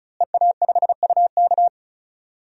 Morse code, heard as a keyed tone, about 36 wpm, 690 Hz. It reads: EA5UK
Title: DXCC-100 Call Signs - 5 Characters - Repeated 3x - 35wpm